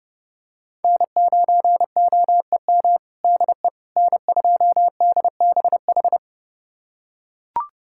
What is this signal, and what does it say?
Morse code, heard as a keyed tone, about 30 wpm, 700 Hz. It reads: N9OEM DE N2DBH
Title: USA Callsign DE Callsign - 5 Characters - 30wpm